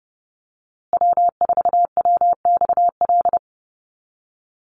Morse code, heard as a keyed tone, about 30 wpm, 700 Hz. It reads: W4WXL
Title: USA Call Signs Repeated 2x 30wpm